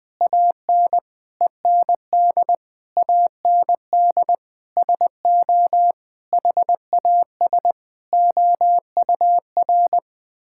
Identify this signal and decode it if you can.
Morse code, heard as a keyed tone, about 20 wpm, 700 Hz. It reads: AN END AND SO HAS OUR
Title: Sentences with 9-Word Phrases - 20wpm